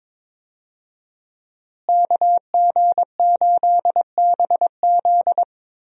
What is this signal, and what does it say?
Morse code, heard as a keyed tone, about 22 wpm, 700 Hz. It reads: KG8BZ
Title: USA Call Signs - 5 Characters - Repeated 2x - 22wpm